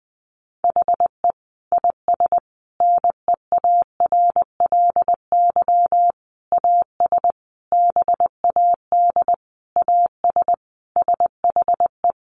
Morse code, heard as a keyed tone, about 20 wpm, 700 Hz. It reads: HE IS NEARLY AS BAD AS SHE
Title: Sentences from Top 1300 Words 20wpm